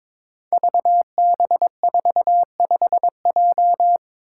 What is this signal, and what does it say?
Morse code, heard as a keyed tone, about 22 wpm, 700 Hz. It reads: VB45J
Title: Code Groups - 5 Letter:Numbers - 22wpm